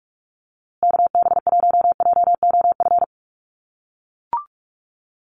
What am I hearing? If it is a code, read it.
Morse code, heard as a keyed tone, about 44 wpm, 690 Hz. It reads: KB1JOF
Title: USA Call Signs Repeated 2x 45wpm